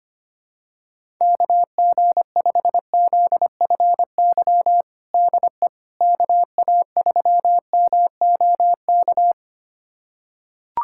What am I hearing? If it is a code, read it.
Morse code, heard as a keyed tone, about 25 wpm, 700 Hz. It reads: KG5ZFY DE KA3MOK
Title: USA Callsign DE Callsign - 6 Characters - 25wpm